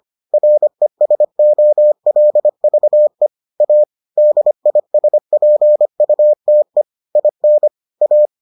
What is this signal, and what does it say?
Morse code, heard as a keyed tone, about 25 wpm, 600 Hz. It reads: RESOLVE A DISPUTE IN A